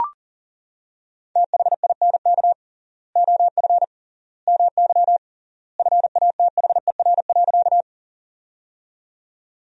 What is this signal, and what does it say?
Morse code, heard as a keyed tone, about 40 wpm, 700 Hz. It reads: THINK OF MY FATHER.
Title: 3x Word-Spacing - Sentences from Top 300 Words - 40wpm